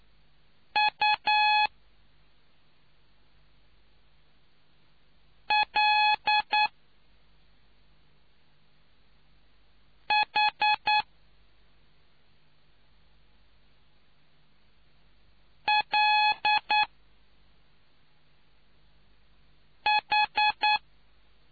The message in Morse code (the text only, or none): U L H L H